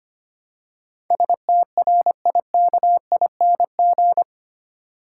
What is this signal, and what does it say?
Morse code, heard as a keyed tone, about 25 wpm, 700 Hz. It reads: STRIKING
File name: Eight Letter Words - 25wpm